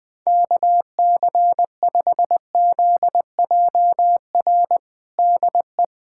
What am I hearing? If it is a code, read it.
Morse code, heard as a keyed tone, about 20 wpm, 700 Hz. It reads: KC5ZJR DE
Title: USA Callsign DE Callsign - 6 Characters - 20wpm